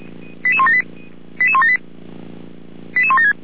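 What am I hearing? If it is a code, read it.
Five-tone selective call (ZVEI), heard as quick runs of five tones, tones of about 70 ms. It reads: 88178 88178 88176
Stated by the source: Five-tone ZVEI selective call, received with an SDR.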